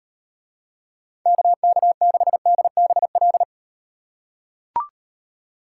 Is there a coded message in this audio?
Yes: Morse code from 1.3 to 3.4 s, heard as a keyed tone, about 38 wpm, 700 Hz. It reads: KK6DBL